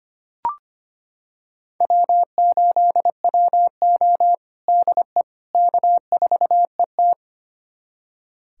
Morse code, heard as a keyed tone, about 25 wpm, 700 Hz. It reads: W8WO DE K4ET
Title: USA Callsign DE Callsign - 4 Characters - 25wpm